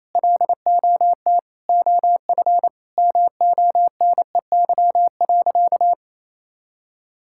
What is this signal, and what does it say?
Morse code, heard as a keyed tone, about 28 wpm, 700 Hz. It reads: LOT OF MONEY.
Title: Sentences from Top 1000 Words - Review Part 5 - 28wpm